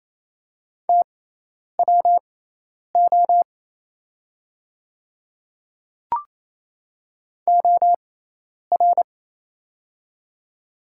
Morse code, heard as a keyed tone, about 28 wpm, spaced as for 10 wpm, 700 Hz. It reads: TWO OR
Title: 6x Character Spacing - Top 100 Words - 28wpm